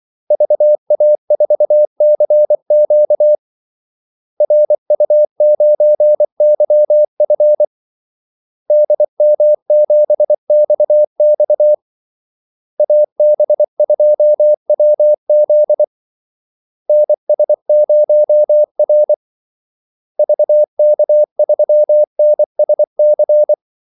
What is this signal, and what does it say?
Morse code, heard as a keyed tone, about 24 wpm, 600 Hz. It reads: VA4CQ RU9YF DM7XX AB2WZ NS0R VK3NSC
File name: Callsigns-24wpm